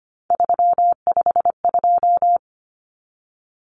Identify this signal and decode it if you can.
Morse code, heard as a keyed tone, about 25 wpm, 700 Hz. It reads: 352
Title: Numbers - 3-Digit 25wpm